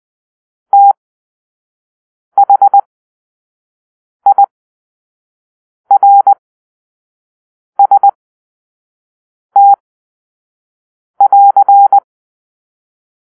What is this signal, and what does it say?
Morse code, heard as a keyed tone, about 20 wpm, spaced as for 5 wpm, 800 Hz. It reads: THIRST<AR>
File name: arrl / Grp-2 Drill-3